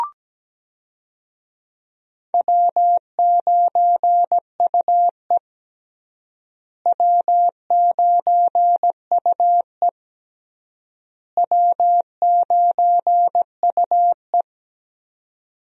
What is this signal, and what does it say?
Morse code, heard as a keyed tone, about 17 wpm, 700 Hz. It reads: W9UE W9UE W9UE
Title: USA Call Signs - 4 Characters - Repeated 3x - 17wpm